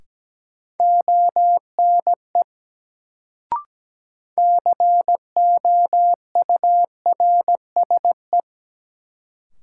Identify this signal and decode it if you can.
Morse code, heard as a keyed tone, about 17 wpm, 700 Hz. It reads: ONE COURSE
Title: Sets of 2 Words - New Word Drill - 17wpm